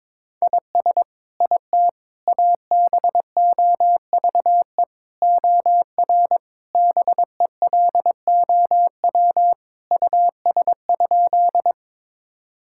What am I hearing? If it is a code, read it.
Morse code, heard as a keyed tone, about 22 wpm, 700 Hz. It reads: IS IT ABOVE OR BELOW US?
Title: Sentences from Top 1000 Words - Review Part 5 - 22wpm